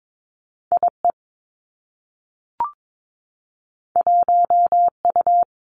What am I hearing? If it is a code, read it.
Morse code, heard as a keyed tone, about 22 wpm, 700 Hz. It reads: I E 1 U
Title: Two Challenging Letter-Number Combinations 22wpm